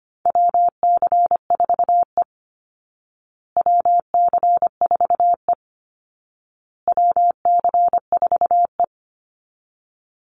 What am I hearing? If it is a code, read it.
Morse code, heard as a keyed tone, about 25 wpm, 700 Hz. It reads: WC4E WC4E WC4E
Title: Call Signs Repeated 2x - 25wpm